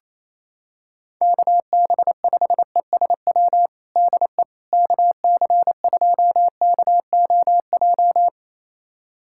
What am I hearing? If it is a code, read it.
Morse code, heard as a keyed tone, about 28 wpm, 700 Hz. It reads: KB5ESW DE KC2KOJ
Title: USA Callsign DE Callsign 28wpm